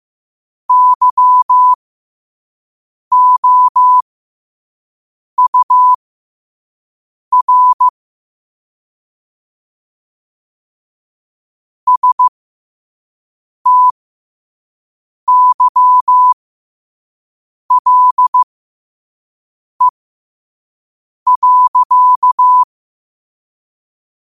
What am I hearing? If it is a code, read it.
Morse code, heard as a keyed tone, about 15 wpm, spaced as for 5 wpm, 1000 Hz. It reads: YOUR STYLE.
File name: qotdfiles / 5wpm